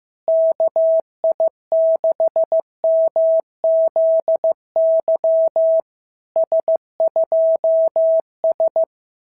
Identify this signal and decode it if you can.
Morse code, heard as a keyed tone, about 14 wpm, 650 Hz. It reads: KI6MZY S2S
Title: SOTA QSO Practice 15wpm